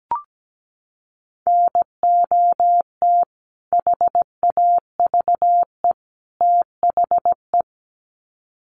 Morse code, heard as a keyed tone, about 17 wpm, 700 Hz. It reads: NOT HAVE THE
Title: Sets of 3 Words 17wpm